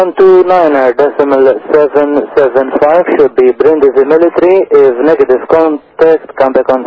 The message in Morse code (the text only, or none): none